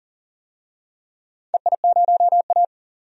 Morse code, heard as a keyed tone, about 40 wpm, 690 Hz. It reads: EI0A
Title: DXCC-100 Call Signs - 4 Characters - 40wpm